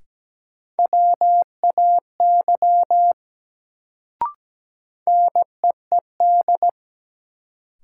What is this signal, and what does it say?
Morse code, heard as a keyed tone, about 17 wpm, 700 Hz. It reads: WAY NEED
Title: Sets of 2 Words - New Word Drill - 17wpm